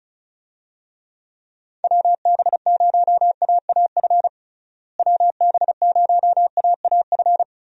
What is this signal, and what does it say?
Morse code, heard as a keyed tone, about 36 wpm, 700 Hz. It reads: WB0AAF WB0AAF
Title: USA Call Signs - 6 Characters - Repeated 3x - 35wpm